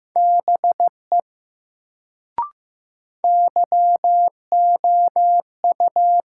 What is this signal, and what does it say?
Morse code, heard as a keyed tone, about 15 wpm, 700 Hz. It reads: BE YOU
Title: Sentences from Top 100 Words - New Word Drill - 15wpm